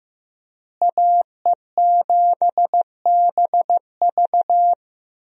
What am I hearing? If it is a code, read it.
Morse code, heard as a keyed tone, about 15 wpm, 700 Hz. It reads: AE7BV
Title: USA Call Signs - 5 Characters - Repeated 3x - 15wpm